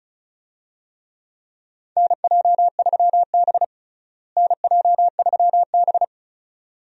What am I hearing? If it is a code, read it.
Morse code, heard as a keyed tone, about 34 wpm, 700 Hz. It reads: NJ3B NJ3B
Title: USA Call Signs - 4 Characters - Repeated 2x - 35wpm